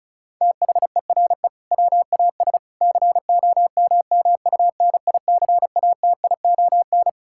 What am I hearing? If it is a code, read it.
Morse code, heard as a keyed tone, about 36 wpm, 700 Hz. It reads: THERE WAS COMMUNICATION
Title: Sentences from Top 1000 Words - Review Part 6 - 35wpm